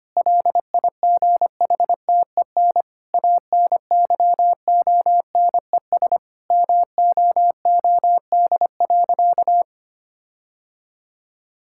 Morse code, heard as a keyed tone, about 25 wpm, 700 Hz. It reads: LIGHTEN ANYONES MOOD.
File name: practice / 25wpm / Difficult Sentences with Hints - 25wpm